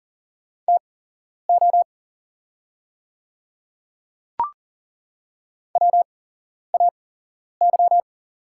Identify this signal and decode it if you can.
Morse code, heard as a keyed tone, about 40 wpm, spaced as for 11 wpm, 690 Hz. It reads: TO WAY
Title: 8x Character Spacing - Top 100 Words - 40wpm